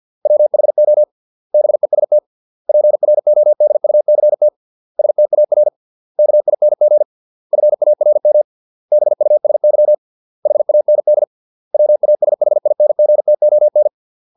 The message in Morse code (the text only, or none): WHO BEST PRODUCT STAR KING FARM BUSY HAND WASHINGTON